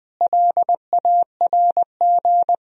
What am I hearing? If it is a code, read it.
Morse code, heard as a keyed tone, about 21 wpm, 700 Hz. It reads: LARG
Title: Sentences from Top 1000 Words - Review Part 1 - 20wpm